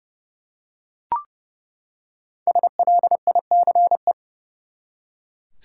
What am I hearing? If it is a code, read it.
Morse code, heard as a keyed tone, about 30 wpm, 700 Hz. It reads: SLICE
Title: Five Letter Words - 30wpm